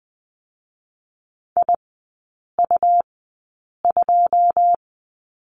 Morse code, heard as a keyed tone, about 20 wpm, spaced as for 8 wpm, 690 Hz. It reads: IU2
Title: 133 Lesson 22e - Question Mark - Three Characters 20wpm